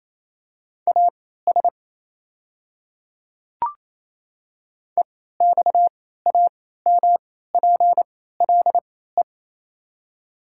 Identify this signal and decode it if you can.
Morse code, heard as a keyed tone, about 27 wpm, spaced as for 16 wpm, 700 Hz. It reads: AS EXAMPLE
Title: 3x Character Spacing - Top 300 Words - 28wpm